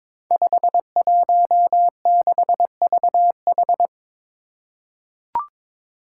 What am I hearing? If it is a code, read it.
Morse code, heard as a keyed tone, about 22 wpm, 700 Hz. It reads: 516VH
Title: Five Challenging Letter-Number Combinations 22wpm